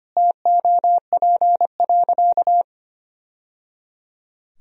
Morse code, heard as a keyed tone, about 25 wpm, 700 Hz. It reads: TOP.
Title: Sentences from Top 1000 Words - Review Part 2 - 25wpm